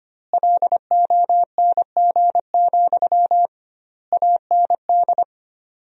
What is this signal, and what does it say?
Morse code, heard as a keyed tone, about 25 wpm, 700 Hz. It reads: LONG, AND
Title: 2x Word-Spacing - Sentences from Top 500 Words - 25wpm